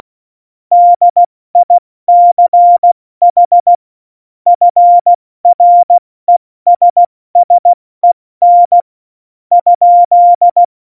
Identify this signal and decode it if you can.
Morse code, heard as a keyed tone, about 16 wpm, spaced as for 13 wpm, 700 Hz. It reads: DICH FRESSEN?
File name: Morsekurs / Witze / Witz 18. September 2018, 70 F80